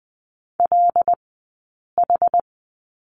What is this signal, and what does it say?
Morse code, heard as a keyed tone, about 20 wpm, spaced as for 8 wpm, 700 Hz. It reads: LH
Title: 204 Lesson 33d - Letter j - Two Characters 20wpm